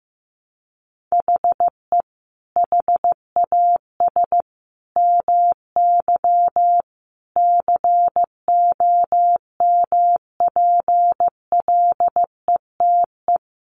Morse code, heard as a keyed tone, about 14 wpm, 700 Hz. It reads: HE HAS MY COMPLETE
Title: Sentences from Top 1500 Words 15wpm